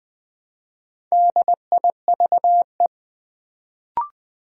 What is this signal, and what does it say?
Morse code, heard as a keyed tone, about 20 wpm, 700 Hz. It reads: DIVE
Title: Build A Word - 1 Letter At A Time - 4-Letter Words - 20wpm